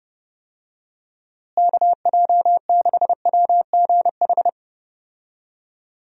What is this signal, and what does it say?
Morse code, heard as a keyed tone, about 30 wpm, 700 Hz. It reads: KJ6WGH
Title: USA Call Signs - 6 Characters - Repeated 2x - 30wpm